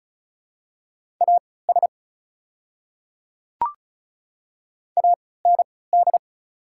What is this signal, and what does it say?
Morse code, heard as a keyed tone, about 36 wpm, spaced as for 20 wpm, 690 Hz. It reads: AS AND
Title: 3x Character Spacing - Top 100 Words - 35wpm